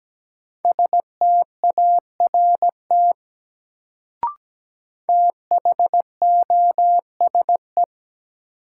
Morse code, heard as a keyed tone, about 17 wpm, 700 Hz. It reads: START THOSE
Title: Sentences from Top 300 Words - New Word Drill - 17wpm